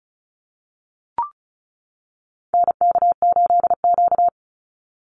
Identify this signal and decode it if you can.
Morse code, heard as a keyed tone, about 34 wpm, 700 Hz. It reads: NK8Q
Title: NA SOTA Activator Callsigns 35wpm